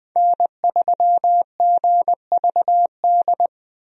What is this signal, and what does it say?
Morse code, heard as a keyed tone, about 20 wpm, 700 Hz. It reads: N3GVD
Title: USA Call Signs - 5 Characters - Repeated 2x - 20wpm